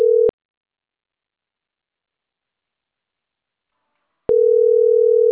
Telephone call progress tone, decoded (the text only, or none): ringback tone